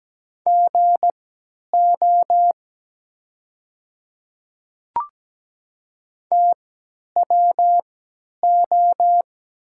Morse code, heard as a keyed tone, about 17 wpm, spaced as for 10 wpm, 700 Hz. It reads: GO TWO
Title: 3x Character Spacing - Top 200 Words - 17wpm